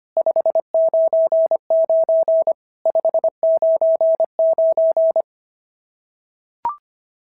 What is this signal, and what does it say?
Morse code, heard as a keyed tone, about 23 wpm, 650 Hz. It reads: 599 599
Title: SOTA QSO Practice 25wpm